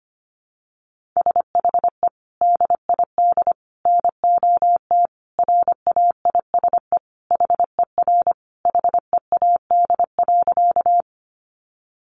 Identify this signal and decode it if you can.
Morse code, heard as a keyed tone, about 25 wpm, 700 Hz. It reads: SHE DID NOT RAISE HER HEAD.
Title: Sentences from Top 1400 Words 25wpm